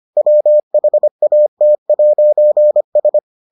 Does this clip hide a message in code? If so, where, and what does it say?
Yes: Morse code from 0.2 to 3.2 s, heard as a keyed tone, about 25 wpm, 600 Hz. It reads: WHAT'S